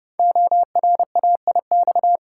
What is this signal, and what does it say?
Morse code, heard as a keyed tone, about 31 wpm, 700 Hz. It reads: ORAIX
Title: Code Groups - 5 Letters - 30wpm